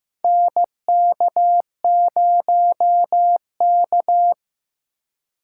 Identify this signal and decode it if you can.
Morse code, heard as a keyed tone, about 15 wpm, 700 Hz. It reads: NK0K
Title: USA Call Signs - 4x Character Spacing - 15wpm